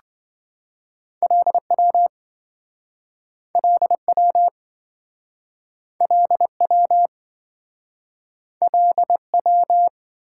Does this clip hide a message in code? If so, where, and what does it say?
Yes: Morse code from 1.2 to 9.9 s, heard as a keyed tone, about 27 wpm, 700 Hz. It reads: LW LW LW LW